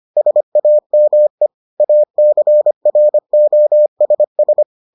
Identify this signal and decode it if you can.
Morse code, heard as a keyed tone, about 25 wpm, 600 Hz. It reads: SAME ACROSS